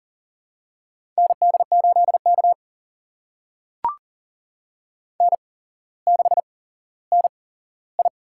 Morse code, heard as a keyed tone, about 40 wpm, spaced as for 17 wpm, 700 Hz. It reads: ND8K N6NI
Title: USA Call Signs - 4 Characters - 8x Character Spacing - 40wpm